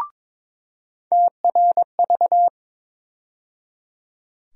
Morse code, heard as a keyed tone, about 22 wpm, 700 Hz. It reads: TRV
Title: USA Call Sign Postfixes - 3 Letters 22wpm